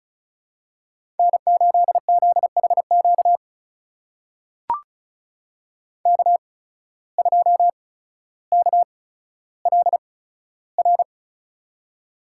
Morse code, heard as a keyed tone, about 34 wpm, spaced as for 15 wpm, 700 Hz. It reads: N8ZHQ K2KLR
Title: USA Call Signs - 5 Characters - 8x Character Spacing - 35wpm